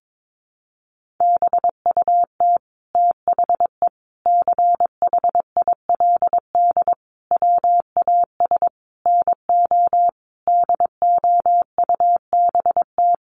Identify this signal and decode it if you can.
Morse code, heard as a keyed tone, about 22 wpm, 700 Hz. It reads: BUT THE CHILD WAS NO DOUBT
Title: Sentences from Top 1500 Words 22wpm